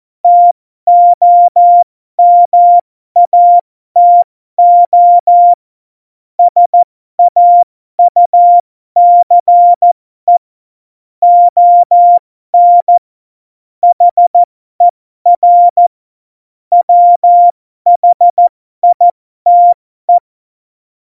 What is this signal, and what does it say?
Morse code, heard as a keyed tone, about 14 wpm, spaced as for 12 wpm, 700 Hz. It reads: TOMATO SAUCE ON HER WHITE
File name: Morsekurs / Witze